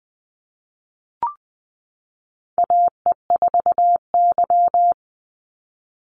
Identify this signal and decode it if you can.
Morse code, heard as a keyed tone, about 20 wpm, 700 Hz. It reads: AE4Y
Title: NA SOTA Chaser Callsigns 20wpm